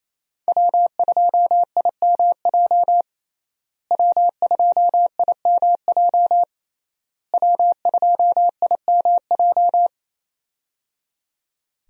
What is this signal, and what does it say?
Morse code, heard as a keyed tone, about 28 wpm, 700 Hz. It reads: W2IMJ W2IMJ W2IMJ
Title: USA Call Signs - 5 Characters - Repeated 3x - 28wpm